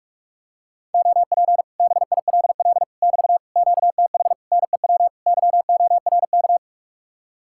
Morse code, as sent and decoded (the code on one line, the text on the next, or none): --- .--. / -... .. .-.. .-.. / -...- / --.- - .... / -. .-- / -.-- --- .-. -.-
OP BILL <BT> QTH NEW YORK